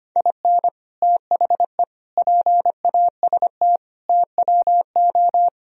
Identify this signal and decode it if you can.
Morse code, heard as a keyed tone, about 26 wpm, 700 Hz. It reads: IN THE PAST TWO
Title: Speed-Racing - Sets of 5 Words 25wpm